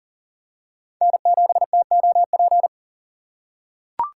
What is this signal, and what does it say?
Morse code, heard as a keyed tone, about 40 wpm, 700 Hz. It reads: N7TOP
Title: USA Call Signs - 2x Character Spacing - 40wpm